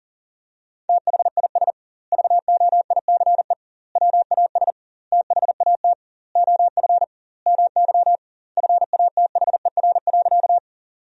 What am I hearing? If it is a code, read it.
Morse code, heard as a keyed tone, about 40 wpm, 700 Hz. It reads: THIS VOICE WAS THAT OF MY FATHER.
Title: 2x Word-Spacing - Sentences from Top 500 Words - 40wpm